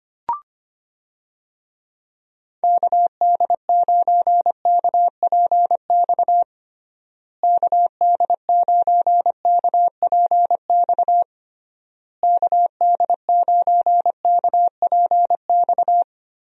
Morse code, heard as a keyed tone, about 25 wpm, 700 Hz. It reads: KD9KPX KD9KPX KD9KPX
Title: USA Call Signs - 6 Characters - Repeated 3x - 25wpm